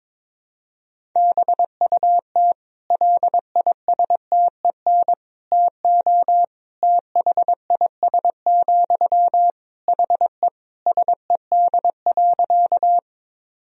Morse code, heard as a keyed tone, about 22 wpm, 700 Hz. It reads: BUT LISTEN TO THIS, HE SED.
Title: Sentences from Top 400 Words and Abbreviations 22wpm